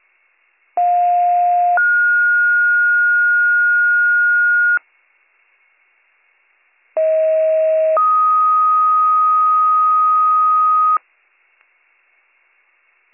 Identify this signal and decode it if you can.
Two-tone page, heard as a one-second tone, then a three-second tone, about 710 then 1430 Hz, about 640 then 1250 Hz. It is heard from 0.8 to 11.0 s.